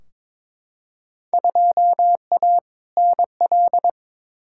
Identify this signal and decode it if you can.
Morse code, heard as a keyed tone, about 21 wpm, 700 Hz. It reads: 2A NL
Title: Field Day Exchanges Encore - 22wpm